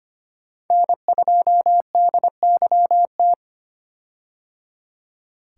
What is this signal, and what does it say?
Morse code, heard as a keyed tone, about 25 wpm, 700 Hz. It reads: N2DYT
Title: K1USN SST QSO Practice - 25wpm